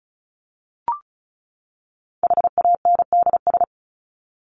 Morse code, heard as a keyed tone, about 34 wpm, 700 Hz. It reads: HANDS
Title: ES:S Suffix - Top 100 Words with Base Words 35wpm